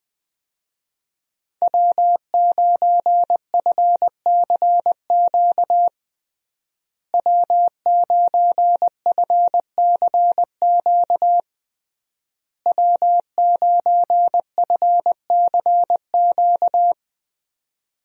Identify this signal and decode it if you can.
Morse code, heard as a keyed tone, about 20 wpm, 700 Hz. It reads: W9FCQ W9FCQ W9FCQ
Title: USA Call Signs - 5 Characters - Repeated 3x - 20wpm